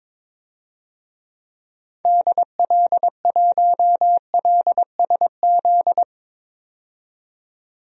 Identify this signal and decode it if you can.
Morse code, heard as a keyed tone, about 22 wpm, 700 Hz. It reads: DL1LSZ
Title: DXCC-100 Call Signs - 6 Characters - 22wpm